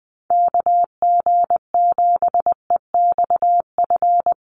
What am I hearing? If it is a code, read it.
Morse code, heard as a keyed tone, about 20 wpm, 700 Hz. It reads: KG7EXF
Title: USA Call Signs Repeated 3x 20wpm